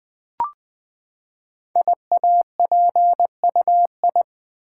Morse code, heard as a keyed tone, about 20 wpm, 700 Hz. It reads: IAPUI
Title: Code Groups - 5 Letters - 20wpm